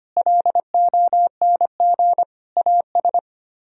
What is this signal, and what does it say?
Morse code, heard as a keyed tone, about 32 wpm, 700 Hz. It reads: LONG AS